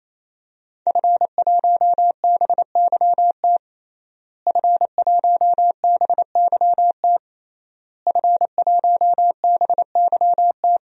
Morse code, heard as a keyed tone, about 27 wpm, 700 Hz. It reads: F1BYT F1BYT F1BYT
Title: DXCC-100 Call Signs - 5 Characters - Repeated 3x - 28wpm